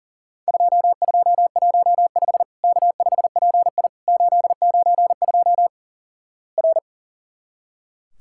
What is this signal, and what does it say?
Morse code, heard as a keyed tone, about 37 wpm, 700 Hz. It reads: 2215 K5PI 892 R
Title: Worked All Europe DX Contest - QTC Exchange - 40wpm